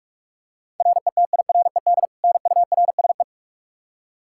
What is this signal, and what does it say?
Morse code, heard as a keyed tone, about 46 wpm, 700 Hz. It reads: RETIRED NURSE
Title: Practice QSO - Exchange 3 - More Info - 45wpm